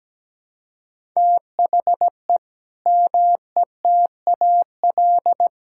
Morse code, heard as a keyed tone, about 17 wpm, 700 Hz. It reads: THE METAL
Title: Sentences from Top 1000 Words - Review Part 6 - 17wpm